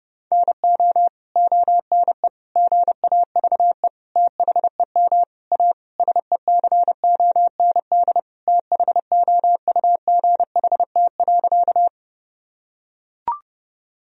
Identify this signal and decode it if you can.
Morse code, heard as a keyed tone, about 30 wpm, 700 Hz. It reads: NO ONE GAVE THEM A SECOND THOUGHT.
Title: Sentences from Top 1000 Words - Review Part 1 - 30wpm